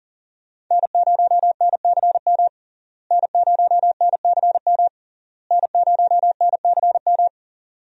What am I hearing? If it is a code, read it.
Morse code, heard as a keyed tone, about 40 wpm, 700 Hz. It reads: N0NCM N0NCM N0NCM
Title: USA Call Signs - 5 Characters - Repeated 3x - 40wpm